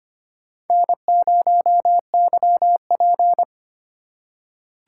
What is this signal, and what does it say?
Morse code, heard as a keyed tone, about 25 wpm, 700 Hz. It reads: N0YP
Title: USA Call Signs - 4 Characters - 3x Character Spacing - 25wpm